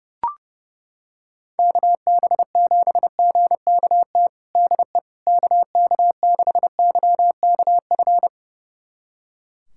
Morse code, heard as a keyed tone, about 30 wpm, 700 Hz. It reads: KB7GKT DE KK6YKF
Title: USA Callsign DE Callsign 30wpm